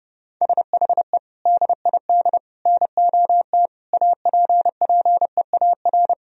Morse code, heard as a keyed tone, about 32 wpm, 700 Hz. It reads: SHE DID NOT APPEAR
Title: Sentences from Top 1000 Words - Review Part 2 - 30wpm